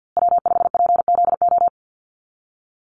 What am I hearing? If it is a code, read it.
Morse code, heard as a keyed tone, about 50 wpm, 700 Hz. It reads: W5PZO
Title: USA Call Signs Repeated 2x 50wpm